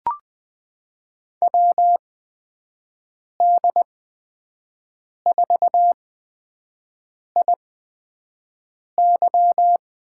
Morse code, heard as a keyed tone, about 21 wpm, spaced as for 5 wpm, 700 Hz. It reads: WD4IY
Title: USA Call Signs - 6 Characters - 8x Character Spacing - 20wpm